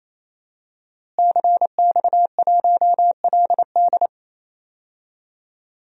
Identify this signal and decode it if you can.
Morse code, heard as a keyed tone, about 28 wpm, 700 Hz. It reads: CX1LD
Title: DXCC-100 Call Signs - 5 Characters - Repeated 3x - 28wpm